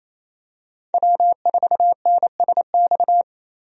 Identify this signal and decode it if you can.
Morse code, heard as a keyed tone, about 27 wpm, 700 Hz. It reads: W4NSX
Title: USA Call Signs - 5 Characters - Repeated 3x - 28wpm